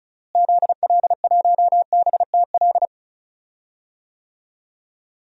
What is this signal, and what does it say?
Morse code, heard as a keyed tone, about 36 wpm, 700 Hz. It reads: ZL1BTL
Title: DXCC-100 Call Signs - 6 Characters - Repeated 3x - 35wpm